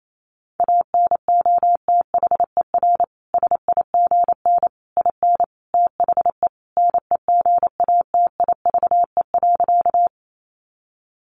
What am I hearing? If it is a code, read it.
Morse code, heard as a keyed tone, about 28 wpm, 700 Hz. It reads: ANOTHER SIGN IN THE NEGATIVE.
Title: Sentences from Top 1000 Words 28wpm